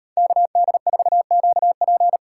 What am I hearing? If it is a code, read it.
Morse code, heard as a keyed tone, about 38 wpm, 700 Hz. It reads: KD4QP